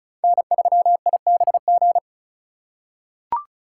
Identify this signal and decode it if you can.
Morse code, heard as a keyed tone, about 36 wpm, 700 Hz. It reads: N3IBG
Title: USA Call Signs - 5 Characters - 6x Character Spacing - 35wpm